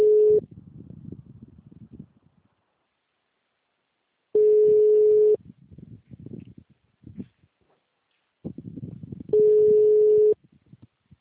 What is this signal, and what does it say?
Ringback tone (Europe, 425 Hz).